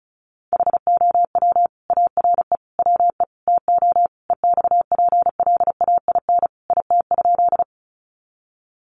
Morse code, heard as a keyed tone, about 34 wpm, 700 Hz. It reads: HOW ARE WE TO EXPLAIN IT?
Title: Sentences from Top 1100 Words 35wpm